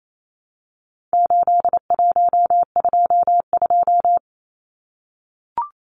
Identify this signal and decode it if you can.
Morse code, heard as a keyed tone, about 28 wpm, 700 Hz. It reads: 8122
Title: Numbers - 4-Digit 28wpm